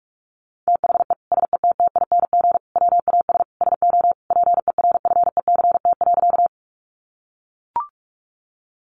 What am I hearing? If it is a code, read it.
Morse code, heard as a keyed tone, about 44 wpm, 700 Hz. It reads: THE SETTING WAS SO PERFECT.
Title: Sentences from Top 1300 Words 45wpm